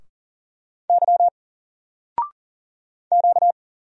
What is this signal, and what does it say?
Morse code, heard as a keyed tone, about 40 wpm, 700 Hz. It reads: Y Q
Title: Single Letter-Number 40wpm